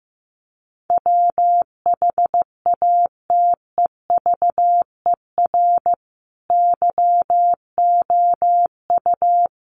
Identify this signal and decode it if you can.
Morse code, heard as a keyed tone, about 15 wpm, 700 Hz. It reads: WHATEVER YOU
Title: Sentences from Top 1100 Words 15wpm